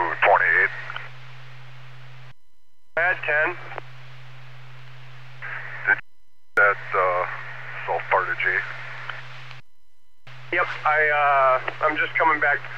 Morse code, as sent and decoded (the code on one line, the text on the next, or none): none